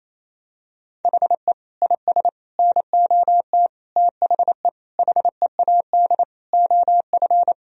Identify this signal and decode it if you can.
Morse code, heard as a keyed tone, about 28 wpm, 700 Hz. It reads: HE IS NOT THE HEAD OF
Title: Sentences from Top 1000 Words - Review Part 2 - 28wpm